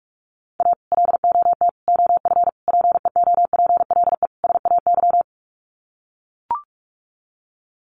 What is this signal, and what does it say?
Morse code, heard as a keyed tone, about 44 wpm, 700 Hz. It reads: A LOT OF PEOPLE SAY
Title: Sets of 5 Words Encore2 45wpm